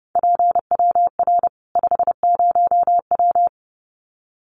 Morse code, heard as a keyed tone, about 30 wpm, 700 Hz. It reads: PWR 50W
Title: Common QSO Phrases 30wpm